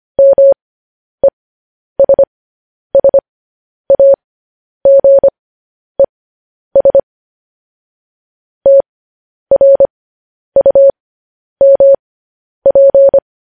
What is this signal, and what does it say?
Morse code, heard as a keyed tone, about 25 wpm, spaced as for 10 wpm, 550 Hz. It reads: MESSAGES TRUMP
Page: news-headlines-in-morse-code-at-10-wpm